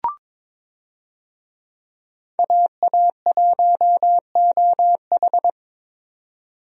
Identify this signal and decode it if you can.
Morse code, heard as a keyed tone, about 22 wpm, 700 Hz. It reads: AA1OH
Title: USA Call Signs - 5 Characters - 22wpm